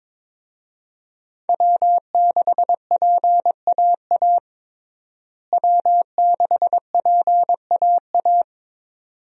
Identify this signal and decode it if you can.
Morse code, heard as a keyed tone, about 22 wpm, 700 Hz. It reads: W6PAA W6PAA
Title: USA Call Signs - 5 Characters - Repeated 2x - 22wpm